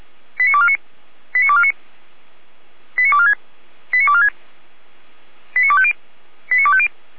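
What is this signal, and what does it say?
Five-tone selective call (ZVEI), heard as quick runs of five tones, tones of about 70 ms. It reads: 89249 89249 89246 89246 89250 89250
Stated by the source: ZVEI five-tone call 89169 received with an SDR.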